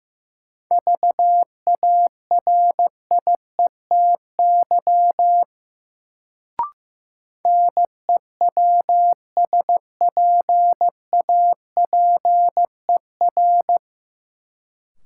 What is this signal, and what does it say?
Morse code, heard as a keyed tone, about 15 wpm, 700 Hz. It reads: VARIETY NEWSPAPER
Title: Sentences from Top 1000 Words - Review Part 4 - New Word Drill - 15wpm